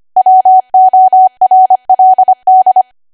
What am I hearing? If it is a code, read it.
Morse code, heard as a keyed tone, about 25 wpm, 750 Hz. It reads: WORLD